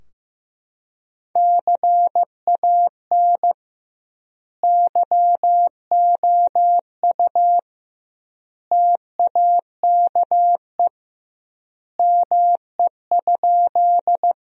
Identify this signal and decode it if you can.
Morse code, heard as a keyed tone, about 15 wpm, 700 Hz. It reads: CAN YOU TAKE ME?
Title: 2x Word-Spacing - Sentences from Top 200 Words - 15wpm